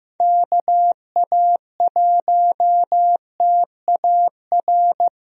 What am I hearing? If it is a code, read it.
Morse code, heard as a keyed tone, about 15 wpm, 700 Hz. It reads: KA1TAR
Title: USA Call Signs - 6 Characters - Repeated 2x - 15wpm